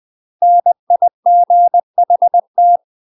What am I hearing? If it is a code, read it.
Morse code, heard as a keyed tone, about 20 wpm, 700 Hz. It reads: NIGHT
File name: UK MASTER QSOs TRAINING / 20WPM / UK MASTER QSOs 0010